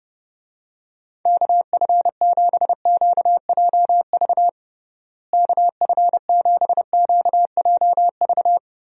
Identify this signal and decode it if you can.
Morse code, heard as a keyed tone, about 30 wpm, 700 Hz. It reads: KF7QJV KF7QJV
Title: USA Call Signs - 6 Characters - Repeated 3x - 30wpm